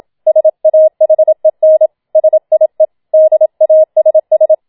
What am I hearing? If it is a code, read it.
Morse code, heard as a keyed tone, about 27 wpm, 630 Hz. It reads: SAHEN SIE DASS